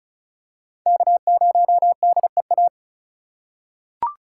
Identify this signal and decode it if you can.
Morse code, heard as a keyed tone, about 34 wpm, 700 Hz. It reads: K0DEA
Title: USA Call Signs - 5 Characters - 4x Character Spacing - 35wpm